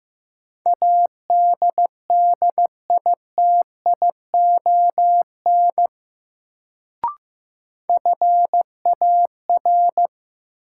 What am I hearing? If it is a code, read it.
Morse code, heard as a keyed tone, about 15 wpm, 700 Hz. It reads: ADDITION FAR
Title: Sets of 3 Words - New Word Drill - 15wpm